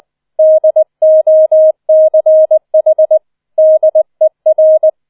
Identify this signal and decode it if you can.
Morse code, heard as a keyed tone, about 20 wpm, 630 Hz. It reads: DOCH DER